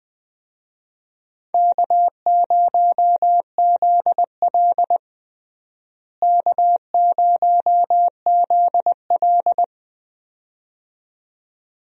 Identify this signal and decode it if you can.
Morse code, heard as a keyed tone, about 20 wpm, 700 Hz. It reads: K0ZL K0ZL
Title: USA Call Signs - 4 Characters - Repeated 2x - 20wpm